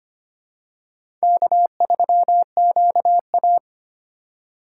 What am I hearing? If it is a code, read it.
Morse code, heard as a keyed tone, about 25 wpm, 690 Hz. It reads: K3QA
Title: USA Call Signs - 4 Characters - 4x Character Spacing - 25wpm